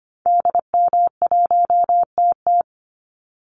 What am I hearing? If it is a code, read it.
Morse code, heard as a keyed tone, about 25 wpm, 700 Hz. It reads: DM1TT
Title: Call Signs Repeated 3x - 25wpm